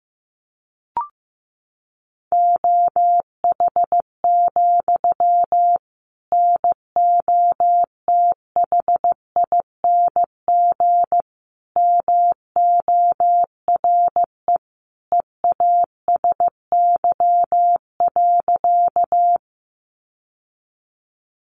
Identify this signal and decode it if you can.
Morse code, heard as a keyed tone, about 15 wpm, 700 Hz. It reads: OH, NOTHING MORE EASY.
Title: Sentences from Top 700 Words 15wpm